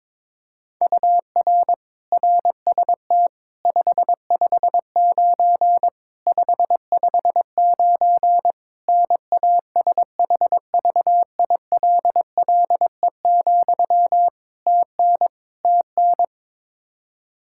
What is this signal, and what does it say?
Morse code, heard as a keyed tone, about 22 wpm, 700 Hz. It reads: UR RST 559 559 NASHVILLE, TN TN
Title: Practice QSO - Exchange 1 - Beginning - 22wpm